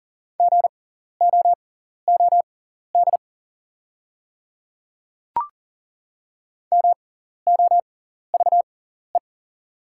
Morse code, heard as a keyed tone, about 40 wpm, spaced as for 14 wpm, 690 Hz. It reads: GOOD MOVE
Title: 6x Character Spacing - Top 300 Words - 40wpm